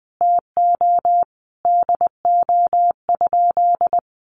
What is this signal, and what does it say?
Morse code, heard as a keyed tone, about 20 wpm, 700 Hz. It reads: TO DO?
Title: Sentences from Top 1100 Words 20wpm